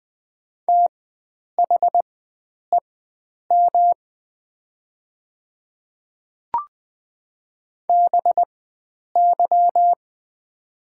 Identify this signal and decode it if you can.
Morse code, heard as a keyed tone, about 20 wpm, spaced as for 9 wpm, 700 Hz. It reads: THEM BY
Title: 4x Character Spacing - Top 100 Words - 20wpm